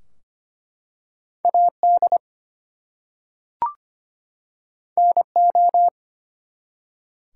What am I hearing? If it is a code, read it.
Morse code, heard as a keyed tone, about 25 wpm, 700 Hz. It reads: AD NO
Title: Two Letter Words - 25wpm